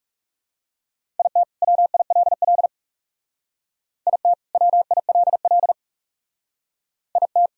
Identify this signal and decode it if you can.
Morse code, heard as a keyed tone, about 39 wpm, 700 Hz. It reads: IT WILL IT WILL IT